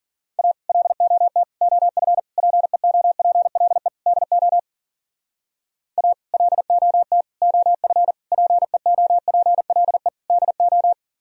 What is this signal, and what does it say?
Morse code, heard as a keyed tone, about 51 wpm, 700 Hz. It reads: A LOT OF PEOPLE DO A LOT OF PEOPLE DO